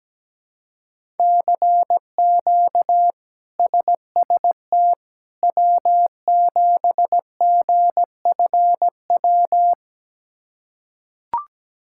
Morse code, heard as a keyed tone, about 17 wpm, 700 Hz. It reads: CQ SST W7GFW
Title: K1USN SST QSO Practice - 17wpm